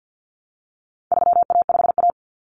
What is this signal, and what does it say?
Morse code, heard as a keyed tone, about 50 wpm, 700 Hz. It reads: 3A5A
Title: Common Cut Numbers - 4-Digit 50wpm